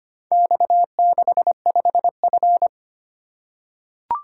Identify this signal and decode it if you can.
Morse code, heard as a keyed tone, about 25 wpm, 700 Hz. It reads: X65F
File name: practice / 25wpm / Four Letter-Number Combinations - 25wpm